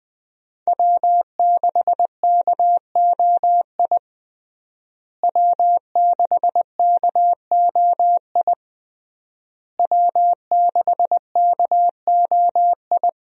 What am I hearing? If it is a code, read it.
Morse code, heard as a keyed tone, about 20 wpm, 700 Hz. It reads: W6KOI W6KOI W6KOI
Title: USA Call Signs - 5 Characters - Repeated 3x - 20wpm